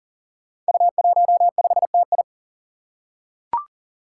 Morse code, heard as a keyed tone, about 41 wpm, 700 Hz. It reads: U15TI
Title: Five Challenging Letter-Number Combinations 40wpm